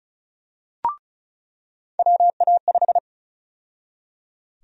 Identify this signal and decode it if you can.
Morse code, heard as a keyed tone, about 36 wpm, 700 Hz. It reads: WA5
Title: USA Call Sign Prefixes - 35wpm